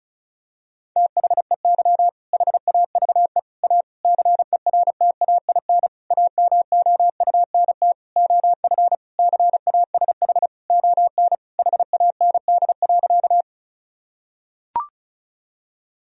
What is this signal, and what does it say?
Morse code, heard as a keyed tone, about 36 wpm, 700 Hz. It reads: THEY HAVE A CERTAIN AMOUNT OF CASH ON HAND.
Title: Sentences from Top 1000 Words - Review Part 4 - 35wpm